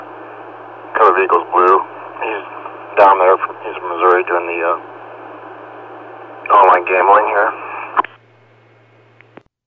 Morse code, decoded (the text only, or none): none